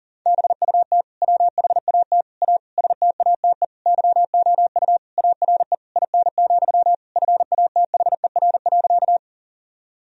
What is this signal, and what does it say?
Morse code, heard as a keyed tone, about 40 wpm, 700 Hz. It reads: BUT WHAT A STATE YOU ARE IN, FATHER.
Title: Sentences from Top 1000 Words - Review Part 2 - 40wpm